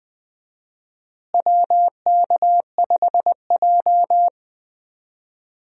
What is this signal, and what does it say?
Morse code, heard as a keyed tone, about 20 wpm, 700 Hz. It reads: WK5J
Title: USA Call Signs - 4 Characters - Repeated 2x - 20wpm